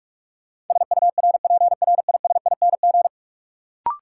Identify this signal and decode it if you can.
Morse code, heard as a keyed tone, about 45 wpm, 700 Hz. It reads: SURPRISING
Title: Sets of 5 Words - New Word Drill - 45wpm